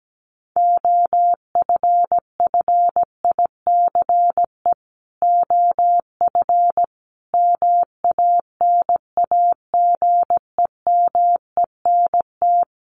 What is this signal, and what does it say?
Morse code, heard as a keyed tone, about 17 wpm, 700 Hz. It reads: OFFICE OF MANAGEMENT
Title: Sets of 5 Words Encore2 17wpm